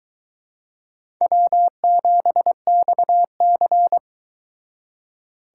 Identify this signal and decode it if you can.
Morse code, heard as a keyed tone, about 23 wpm, 700 Hz. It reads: W7XC